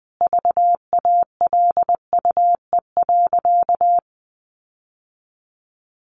Morse code, heard as a keyed tone, about 20 wpm, 700 Hz. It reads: VALUE.
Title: Sentences from Top 1400 Words 20wpm